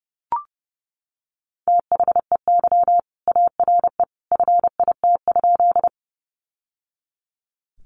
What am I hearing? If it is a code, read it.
Morse code, heard as a keyed tone, about 30 wpm, 700 Hz. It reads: THEY ARE FIT?
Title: Sentences from Top 800 Words 30wpm